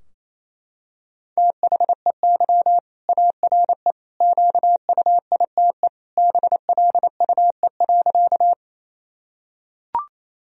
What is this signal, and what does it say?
Morse code, heard as a keyed tone, about 28 wpm, 700 Hz. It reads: THEY ARE QUITE BLUE.
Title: Sentences from Top 1000 Words - Review Part 3 - 28wpm